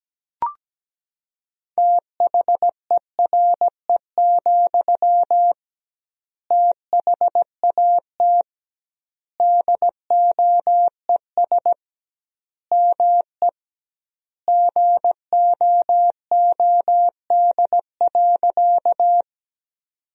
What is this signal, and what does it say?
Morse code, heard as a keyed tone, about 17 wpm, 700 Hz. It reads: THERE, THAT DOES ME GOOD.
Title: 2x Word-Spacing - Sentences from Top 200 Words - 17wpm